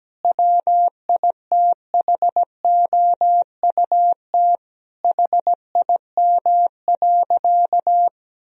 Morse code, heard as a keyed tone, about 17 wpm, 700 Hz. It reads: WITHOUT HIM.
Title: Sentences from Top 1000 Words - Review Part 3 - 17wpm